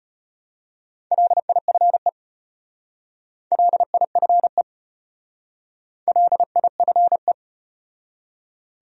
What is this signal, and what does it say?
Morse code, heard as a keyed tone, about 32 wpm, 700 Hz. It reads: LIFE LIFE LIFE